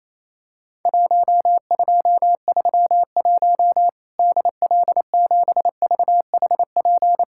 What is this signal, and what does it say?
Morse code, heard as a keyed tone, about 28 wpm, 700 Hz. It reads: 1231 DL7VHP
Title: Worked All Europe DX Contest - QTC Exchange - 28wpm